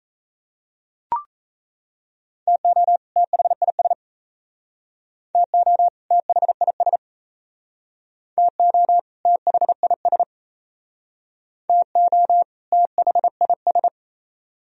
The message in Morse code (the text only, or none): TO THIS TO THIS TO THIS TO THIS